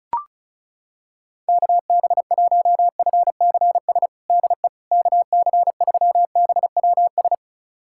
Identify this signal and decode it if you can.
Morse code, heard as a keyed tone, about 36 wpm, 700 Hz. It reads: KB1FCS DE KC3BWS
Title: USA Callsign DE Callsign 35wpm